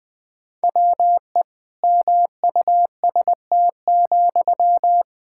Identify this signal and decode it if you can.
Morse code, heard as a keyed tone, about 20 wpm, 700 Hz. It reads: WE MUST,
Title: Sentences from Top 1600 Words 20wpm